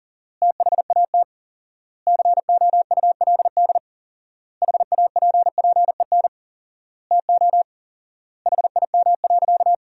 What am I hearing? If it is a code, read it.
Morse code, heard as a keyed tone, about 37 wpm, 700 Hz. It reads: THAT COULD HAPPEN TO HIM.
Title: 4x Word-Spacing - Sentences from Top 400 Words - 40wpm